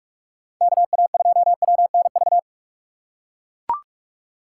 Morse code, heard as a keyed tone, about 43 wpm, 700 Hz. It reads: KA2WNV
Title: USA Call Signs - 2x Character Spacing - 45wpm